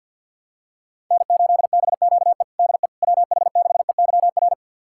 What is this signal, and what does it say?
Morse code, heard as a keyed tone, about 52 wpm, 700 Hz. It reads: N8BQE DE WH6EYR
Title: USA Callsign DE Callsign 50wpm